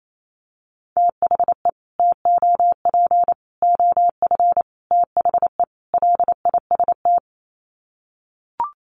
Morse code, heard as a keyed tone, about 28 wpm, 700 Hz. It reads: THE TOP OF THE LIST
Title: Sets of 5 Words Encore2 28wpm